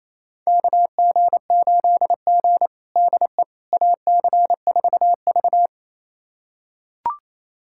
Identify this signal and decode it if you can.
Morse code, heard as a keyed tone, about 27 wpm, 700 Hz. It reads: KG8G DE AC4V
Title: USA Callsign DE Callsign - 4 Characters - 28wpm